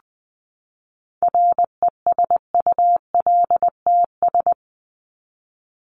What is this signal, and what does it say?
Morse code, heard as a keyed tone, about 20 wpm, 700 Hz. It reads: RESULTS
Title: ES:S Suffix - Top 100 Words with Base Words 20wpm